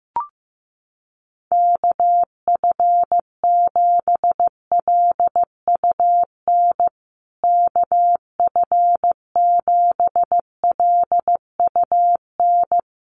Morse code, heard as a keyed tone, about 15 wpm, 700 Hz. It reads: KF7LUN KF7LUN
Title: USA Call Signs Repeated 3x 15wpm